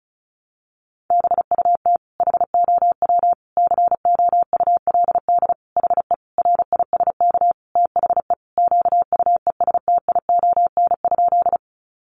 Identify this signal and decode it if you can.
Morse code, heard as a keyed tone, about 34 wpm, 700 Hz. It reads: BUT HOW COULD HE RISK THE QUESTION?
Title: Sentences from Top 600 Words 35wpm